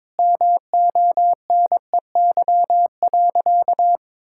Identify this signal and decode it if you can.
Morse code, heard as a keyed tone, about 22 wpm, 700 Hz. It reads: MONEY.
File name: Difficult Sentences with Hints - 22wpm